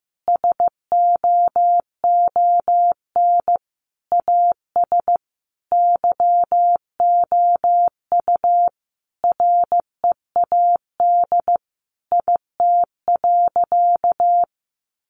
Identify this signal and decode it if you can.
Morse code, heard as a keyed tone, about 15 wpm, 700 Hz. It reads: SOON AS YOU READ IT.
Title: Sentences from Top 1300 Words 15wpm